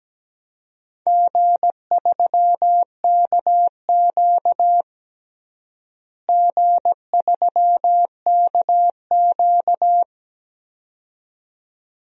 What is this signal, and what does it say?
Morse code, heard as a keyed tone, about 17 wpm, 700 Hz. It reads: G3KQ G3KQ
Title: DXCC-100 Call Signs - 4 Characters - Repeated 2x - 17wpm